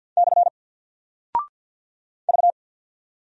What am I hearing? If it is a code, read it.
Morse code, heard as a keyed tone, about 48 wpm, 700 Hz. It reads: / V
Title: Single Character (Letter:Number:Punctuation:Prosign) Alt 50wpm